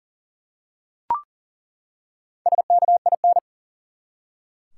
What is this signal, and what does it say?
Morse code, heard as a keyed tone, about 40 wpm, 700 Hz. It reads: SKIN
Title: Four Letter Words - 40wpm